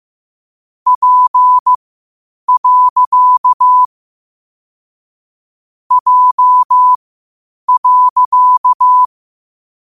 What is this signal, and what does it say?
Morse code, heard as a keyed tone, about 15 wpm, spaced as for 8 wpm, 1000 Hz. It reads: P. J.
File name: qotdfiles / 8wpm